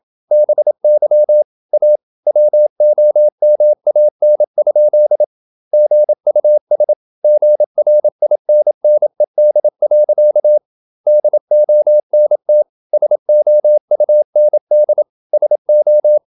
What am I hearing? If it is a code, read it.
Morse code, heard as a keyed tone, about 27 wpm, 600 Hz. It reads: BY A WOMAN? GUS GRINNED. DONT SOUND SO
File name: Never 0027wpm